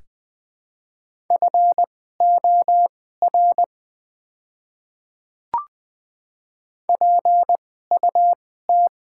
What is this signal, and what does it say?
Morse code, heard as a keyed tone, about 20 wpm, spaced as for 14 wpm, 700 Hz. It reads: FOR PUT
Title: 2x Character Spacing - Top 200 Words - 20wpm